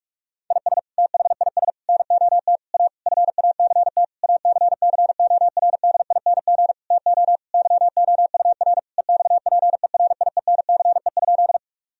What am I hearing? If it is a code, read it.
Morse code, heard as a keyed tone, about 47 wpm, 700 Hz. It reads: IS THIS NOT A FACT ACCORDING TO YOUR EXPERIENCE?
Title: Sentences from Top 1000 Words - Review Part 4 - 45wpm